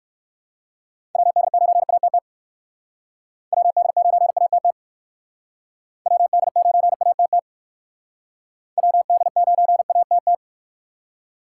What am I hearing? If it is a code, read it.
Morse code, heard as a keyed tone, about 56 wpm, 700 Hz. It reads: WD9ATT WD9ATT WD9ATT WD9ATT